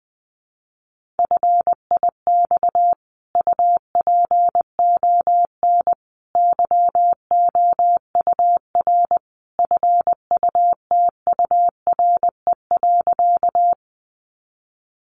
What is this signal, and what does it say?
Morse code, heard as a keyed tone, about 20 wpm, 700 Hz. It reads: FIX UPON YOUR FUTURE.
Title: Sentences from Top 1400 Words 20wpm